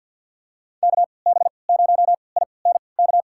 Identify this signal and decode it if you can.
Morse code, heard as a keyed tone, about 47 wpm, spaced as for 28 wpm, 700 Hz. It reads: KB0INK
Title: USA Call Signs - 6 Characters - 3x Character Spacing - 50wpm